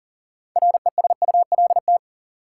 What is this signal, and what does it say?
Morse code, heard as a keyed tone, about 40 wpm, 700 Hz. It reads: RESULT
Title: Sets of 3 Words - New Word Drill - 40wpm